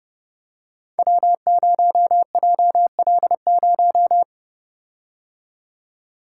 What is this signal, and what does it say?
Morse code, heard as a keyed tone, about 30 wpm, 700 Hz. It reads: W0JL0
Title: Code Groups - 5 Letter:Numbers - 30wpm